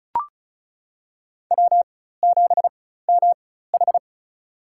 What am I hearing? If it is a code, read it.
Morse code, heard as a keyed tone, about 36 wpm, spaced as for 16 wpm, 700 Hz. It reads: W7MH
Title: USA Call Signs - 4 Characters - 4x Character Spacing - 35wpm